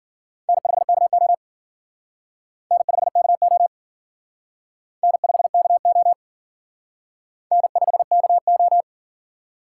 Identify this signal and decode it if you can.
Morse code, heard as a keyed tone, about 51 wpm, 700 Hz. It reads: N5KO N5KO N5KO N5KO